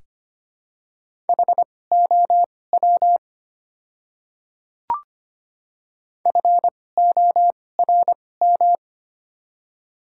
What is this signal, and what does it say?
Morse code, heard as a keyed tone, about 25 wpm, spaced as for 18 wpm, 700 Hz. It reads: HOW FORM
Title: 2x Character Spacing - Top 200 Words - 25wpm